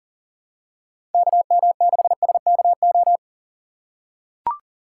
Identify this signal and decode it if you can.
Morse code, heard as a keyed tone, about 40 wpm, 700 Hz. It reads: KM6SKO
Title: USA Call Signs - 6 Characters - 2x Character Spacing - 40wpm